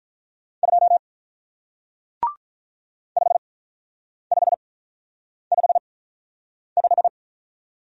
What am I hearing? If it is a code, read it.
Morse code, heard as a keyed tone, about 47 wpm, spaced as for 9 wpm, 700 Hz. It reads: 2 5555